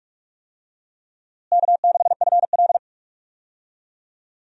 Morse code, heard as a keyed tone, about 46 wpm, 700 Hz. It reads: K6FL
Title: USA Call Signs - 4 Characters - Repeated 3x - 45wpm